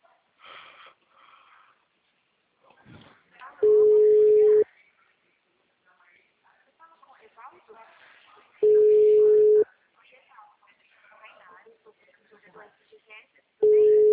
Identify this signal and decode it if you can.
Ringback tone (Europe, 425 Hz), with speech.